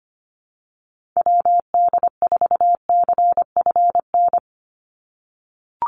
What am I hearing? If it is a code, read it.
Morse code, heard as a keyed tone, about 25 wpm, 700 Hz. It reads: WD4CFN
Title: NA SOTA Chaser Callsigns 25wpm